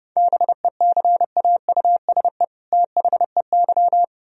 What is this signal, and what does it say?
Morse code, heard as a keyed tone, about 38 wpm, 700 Hz. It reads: BECAUSE THEY